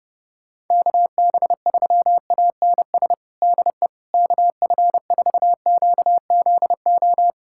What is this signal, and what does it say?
Morse code, heard as a keyed tone, about 32 wpm, 700 Hz. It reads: KB3ANS DE KF4QZO
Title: USA Callsign DE Callsign - 6 Characters - 30wpm